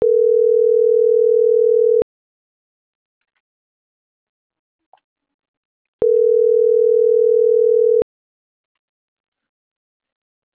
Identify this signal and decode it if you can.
Ringback tone (North America), 440 + 480 Hz.